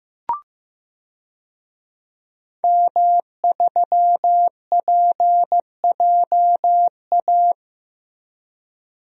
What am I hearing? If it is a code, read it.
Morse code, heard as a keyed tone, about 15 wpm, 700 Hz. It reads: M3PJA
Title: DXCC-100 Call Signs - 5 Characters - Repeated 2x - 15wpm